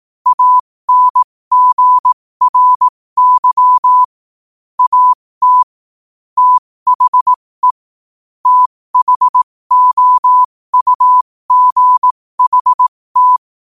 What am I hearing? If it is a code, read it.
Morse code, heard as a keyed tone, about 18 wpm, spaced as for 15 wpm, 1000 Hz. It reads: ANGRY AT THE THOUGHT
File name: qotdfiles / 15wpm